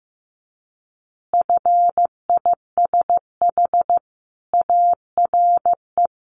Morse code, heard as a keyed tone, about 15 wpm, 700 Hz. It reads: FISH ARE
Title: Easy Phrases 15wpm